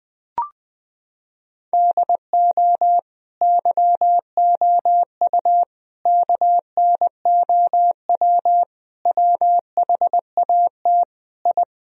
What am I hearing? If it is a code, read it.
Morse code, heard as a keyed tone, about 20 wpm, 700 Hz. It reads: DO YOU KNOW WHAT I
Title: Sentences from Top 1000 Words - Review Part 4 - 20wpm